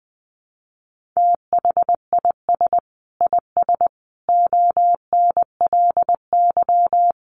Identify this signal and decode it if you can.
Morse code, heard as a keyed tone, about 19 wpm, 700 Hz. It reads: THIS IS ONLY
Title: Sentences from Top 1400 Words 20wpm